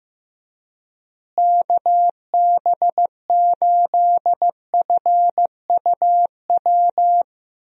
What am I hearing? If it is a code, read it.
Morse code, heard as a keyed tone, about 15 wpm, 700 Hz. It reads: KB8FUW
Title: USA Call Signs - 6 Characters - Repeated 3x - 15wpm